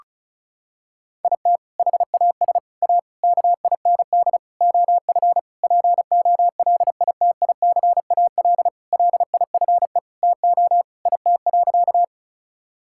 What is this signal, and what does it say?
Morse code, heard as a keyed tone, about 36 wpm, 700 Hz. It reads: IT HAS A KIND OF POLITICAL LIFE TO IT.
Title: Sentences from Top 1000 Words - Review Part 5 - 35wpm